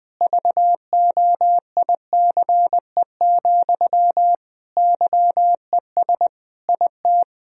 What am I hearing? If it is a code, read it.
Morse code, heard as a keyed tone, about 21 wpm, 700 Hz. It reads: VOICE, YES IT
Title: Sentences from Top 1000 Words - Review Part 1 - 20wpm